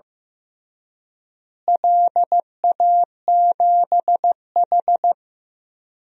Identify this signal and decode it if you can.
Morse code, heard as a keyed tone, about 15 wpm, 700 Hz. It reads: LA7H
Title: DXCC-100 Call Signs - 4 Characters - Repeated 3x - 15wpm